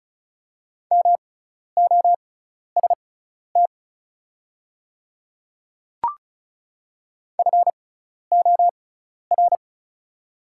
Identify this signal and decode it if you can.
Morse code, heard as a keyed tone, about 36 wpm, spaced as for 12 wpm, 690 Hz. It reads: MOST FOR
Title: 6x Character Spacing - Top 300 Words - 35wpm